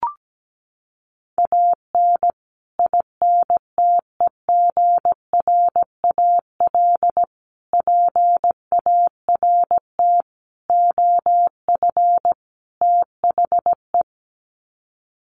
Morse code, heard as a keyed tone, about 17 wpm, 700 Hz. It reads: AN INTEGRAL PART OF THE
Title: Sets of 5 Words 17wpm